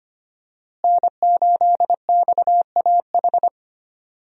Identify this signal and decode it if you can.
Morse code, heard as a keyed tone, about 25 wpm, 700 Hz. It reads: N8XAH
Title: USA Call Signs - 5 Characters - Repeated 3x - 25wpm